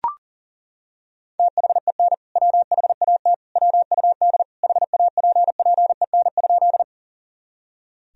Morse code, heard as a keyed tone, about 40 wpm, 700 Hz. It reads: THEN WHAT WUD HAPPEN?
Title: Sentences from Top 400 Words and Abbreviations 40wpm